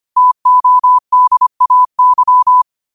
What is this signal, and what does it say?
Morse code, heard as a keyed tone, about 25 wpm, 1000 Hz. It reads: TODAY
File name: qotdfiles / 25wpm